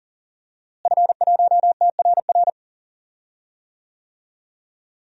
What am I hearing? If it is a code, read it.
Morse code, heard as a keyed tone, about 40 wpm, 700 Hz. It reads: F1TRR
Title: DXCC-100 Call Signs - 5 Characters - Repeated 3x - 40wpm